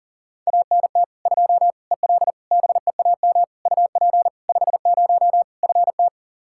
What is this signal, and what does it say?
Morse code, heard as a keyed tone, about 40 wpm, 700 Hz. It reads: ANT 2 EL BEAM UP 50 FT
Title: Practice QSO - Exchange 2 - Info - 40wpm